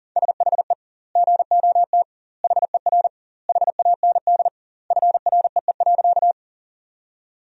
Morse code, heard as a keyed tone, about 41 wpm, 700 Hz. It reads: SHE GOT HER HAND FREE.
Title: 2x Word-Spacing - Sentences from Top 500 Words - 40wpm